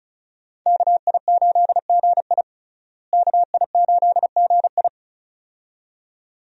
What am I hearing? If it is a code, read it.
Morse code, heard as a keyed tone, about 36 wpm, 700 Hz. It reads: KI8GI KI8GI
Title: USA Call Signs - 5 Characters - Repeated 2x - 35wpm